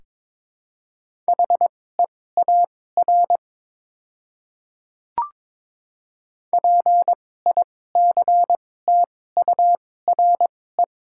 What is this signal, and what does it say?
Morse code, heard as a keyed tone, about 22 wpm, spaced as for 16 wpm, 700 Hz. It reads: HEAR PICTURE
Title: 2x Character Spacing - Top 300 Words - 22wpm